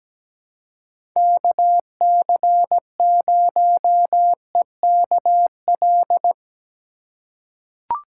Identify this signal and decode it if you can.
Morse code, heard as a keyed tone, about 17 wpm, 700 Hz. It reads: KC0EKL
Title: USA Call Signs - 6 Characters - 4x Character Spacing - 17wpm